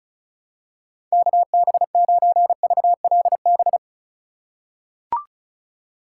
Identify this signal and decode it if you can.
Morse code, heard as a keyed tone, about 34 wpm, 700 Hz. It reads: KB9VLB
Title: USA Call Signs - 6 Characters - 4x Character Spacing - 35wpm